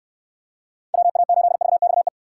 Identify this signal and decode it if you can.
Morse code, heard as a keyed tone, about 66 wpm, 700 Hz. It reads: WA8VXE